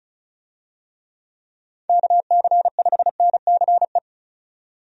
Morse code, heard as a keyed tone, about 34 wpm, 700 Hz. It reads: KC5NCE
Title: USA Call Signs - 6 Characters - 35wpm